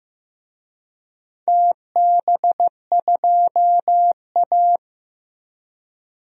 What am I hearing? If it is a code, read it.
Morse code, heard as a keyed tone, about 16 wpm, 700 Hz. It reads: TB2A
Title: DXCC-100 Call Signs - 4 Characters - Repeated 3x - 15wpm